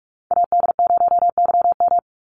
Morse code, heard as a keyed tone, about 44 wpm, 700 Hz. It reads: AD0YM
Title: NA SOTA Chaser Callsigns 45wpm